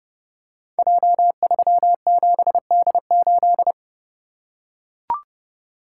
Morse code, heard as a keyed tone, about 30 wpm, 700 Hz. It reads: J37D8
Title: Code Groups - 5 Letter:Numbers - 30wpm